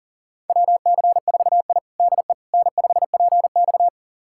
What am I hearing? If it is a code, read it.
Morse code, heard as a keyed tone, about 37 wpm, 700 Hz. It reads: WC4I DE N5PX
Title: USA Callsign DE Callsign - 4 Characters - 40wpm